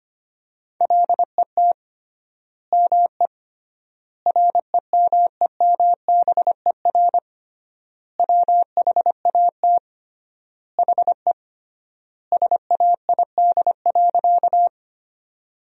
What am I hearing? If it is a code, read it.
Morse code, heard as a keyed tone, about 25 wpm, 700 Hz. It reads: LET ME REMEMBER WHAT HE SAID.
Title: 3x Word-Spacing - Sentences from Top 400 Words - 25wpm